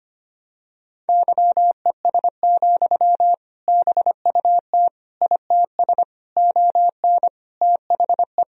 Morse code, heard as a keyed tone, about 25 wpm, 700 Hz. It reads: YES, BUT ITS ON THE
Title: Sentences from Top 1000 Words - Review Part 1 - 25wpm